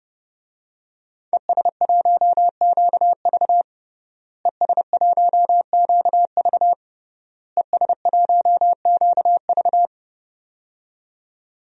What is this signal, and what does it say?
Morse code, heard as a keyed tone, about 32 wpm, 700 Hz. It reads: ES1QV ES1QV ES1QV
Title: DXCC-100 Call Signs - 5 Characters - Repeated 3x - 30wpm